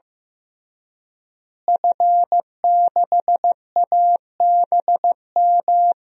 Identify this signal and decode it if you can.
Morse code, heard as a keyed tone, about 15 wpm, 700 Hz. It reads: F6ABM
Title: DXCC-100 Call Signs - 5 Characters - Repeated 3x - 15wpm